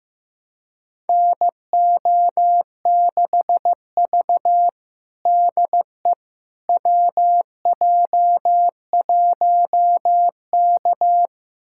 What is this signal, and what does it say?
Morse code, heard as a keyed tone, about 15 wpm, 700 Hz. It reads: NO6V DE WJ1K
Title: USA Callsign DE Callsign - 4 Characters - 15wpm